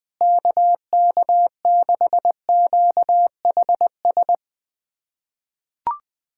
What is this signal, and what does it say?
Morse code, heard as a keyed tone, about 21 wpm, 700 Hz. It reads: KK6QHS
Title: USA Callsign DE Callsign 20wpm